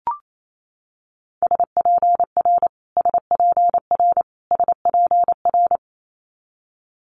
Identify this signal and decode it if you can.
Morse code, heard as a keyed tone, about 28 wpm, 700 Hz. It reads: SPR SPR SPR
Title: Multi-letter Phonemes Repeated 3x 28wpm